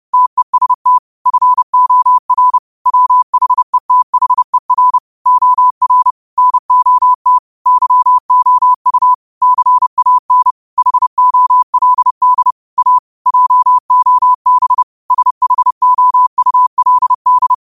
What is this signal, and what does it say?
Morse code, heard as a keyed tone, about 30 wpm, 1000 Hz. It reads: TEST FOR WHETHER OR NOT YOU CAN HOLD A JOB SHOULD
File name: qotdfiles / 30wpm